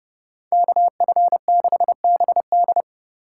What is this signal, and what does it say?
Morse code, heard as a keyed tone, about 31 wpm, 700 Hz. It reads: KF6BD